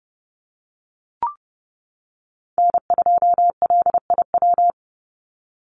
Morse code, heard as a keyed tone, about 30 wpm, 700 Hz. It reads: N2LIW
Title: USA Call Signs 30wpm